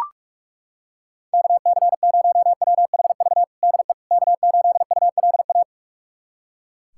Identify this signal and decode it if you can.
Morse code, heard as a keyed tone, about 43 wpm, 700 Hz. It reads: KC0WHV DE K8ULA
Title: USA Callsign DE Callsign 45wpm